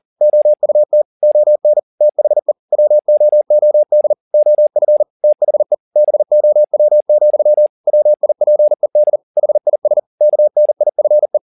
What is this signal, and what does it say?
Morse code, heard as a keyed tone, about 40 wpm, 600 Hz. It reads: OUT ON THE WOOD OF THE BOW, WIPED HIS KNIFE